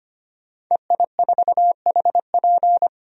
Morse code, heard as a keyed tone, about 25 wpm, 700 Hz. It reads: EI4HP
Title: DXCC-100 Call Signs - 5 Characters - 25wpm